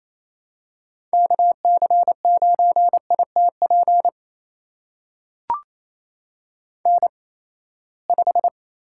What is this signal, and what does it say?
Morse code, heard as a keyed tone, about 28 wpm, spaced as for 19 wpm, 700 Hz. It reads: KC9ITP N5
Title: USA Call Signs - 8x Character Spacing - 28wpm